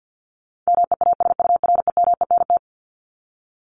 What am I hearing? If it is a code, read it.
Morse code, heard as a keyed tone, about 50 wpm, 700 Hz. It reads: MEASUREMENT
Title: Top 1751-2000 Words 50wpm